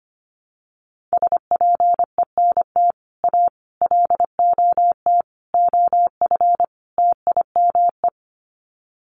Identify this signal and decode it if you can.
Morse code, heard as a keyed tone, about 25 wpm, 700 Hz. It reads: SPENT A LOT OF TIME
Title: Sets of 5 Words 25wpm